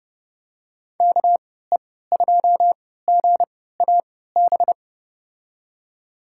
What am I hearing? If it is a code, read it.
Morse code, heard as a keyed tone, about 30 wpm, spaced as for 17 wpm, 700 Hz. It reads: KE2GAB
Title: USA Call Signs - 6 Characters - 3x Character Spacing - 30wpm